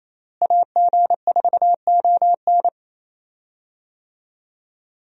Morse code, heard as a keyed tone, about 28 wpm, 700 Hz. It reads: AG4ON
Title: USA Call Signs - 5 Characters - Repeated 2x - 28wpm